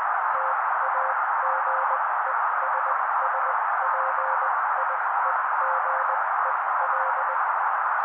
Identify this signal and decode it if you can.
Morse code, heard as a keyed tone, about 20 wpm, spaced as for 16 wpm, 530 Hz. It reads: TAGESSPIEGEL
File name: daily / 16wpm noise